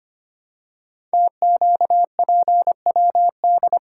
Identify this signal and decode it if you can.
Morse code, heard as a keyed tone, about 25 wpm, 700 Hz. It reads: TQPWD
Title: Code Groups - 5 Letters - 25wpm